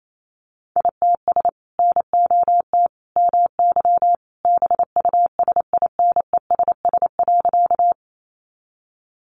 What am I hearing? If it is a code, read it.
Morse code, heard as a keyed tone, about 28 wpm, 700 Hz. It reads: ITS NOT MY BUSINESS.
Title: Sentences from Top 600 Words 28wpm